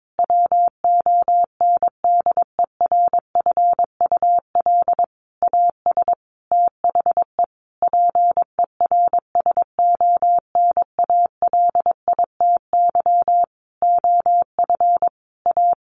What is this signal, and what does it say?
Morse code, heard as a keyed tone, about 22 wpm, 700 Hz. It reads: WONDERFUL AS THE PERSONALITY OF A
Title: Sentences from Top 1400 Words 22wpm